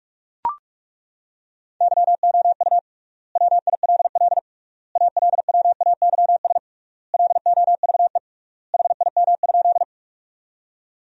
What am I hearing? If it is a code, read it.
Morse code, heard as a keyed tone, about 41 wpm, 700 Hz. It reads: YOU WILL ALWAYS LOVE HIM?
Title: 3x Word-Spacing - Sentences from Top 400 Words - 45wpm